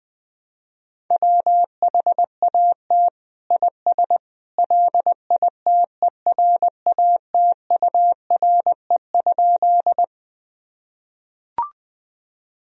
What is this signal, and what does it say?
Morse code, heard as a keyed tone, about 21 wpm, 700 Hz. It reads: WHAT IS LITERATURE?
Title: Sentences from Top 1000 Words - Review Part 3 - 20wpm